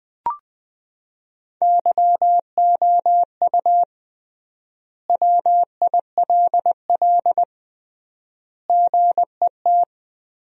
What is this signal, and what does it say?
Morse code, heard as a keyed tone, about 20 wpm, 700 Hz. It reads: YOU WILL GET
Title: 3x Word-Spacing - Sentences from Top 200 Words - 20wpm